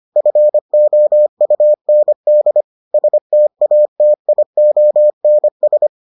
Morse code, heard as a keyed tone, about 25 wpm, 600 Hz. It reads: FOUND STATIONS